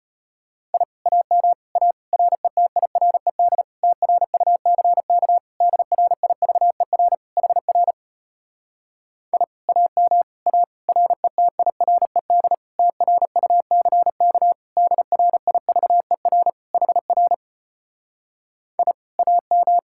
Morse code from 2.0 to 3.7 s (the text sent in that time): RETIRED